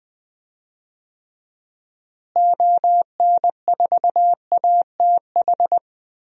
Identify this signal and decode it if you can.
Morse code, heard as a keyed tone, about 20 wpm, 700 Hz. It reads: ON4ATH
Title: DXCC-100 Call Signs - 6 Characters - Repeated 2x - 20wpm